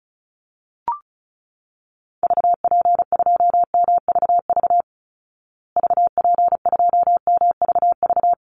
Morse code, heard as a keyed tone, about 34 wpm, 700 Hz. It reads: VP2MVV VP2MVV
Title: Call Signs Repeated 3x - 35wpm